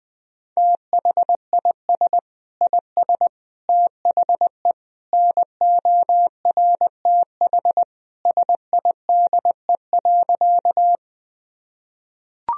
Morse code, heard as a keyed tone, about 20 wpm, 700 Hz. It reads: THIS IS THE NORTH SIDE.
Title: Sentences from Top 1000 Words - Review Part 1 - 20wpm